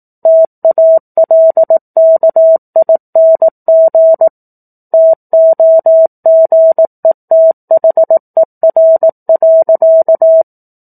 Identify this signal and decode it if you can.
Morse code, heard as a keyed tone, about 18 wpm, 650 Hz. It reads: TALKING TOGETHER.